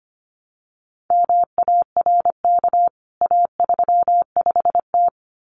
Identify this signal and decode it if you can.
Morse code, heard as a keyed tone, about 25 wpm, 700 Hz. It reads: MARK A35T
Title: CWT Exchange 25wpm